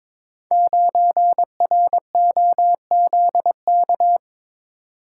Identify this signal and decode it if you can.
Morse code, heard as a keyed tone, about 21 wpm, 700 Hz. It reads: 9ROZK
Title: Code Groups - 5 Letter:Numbers - 22wpm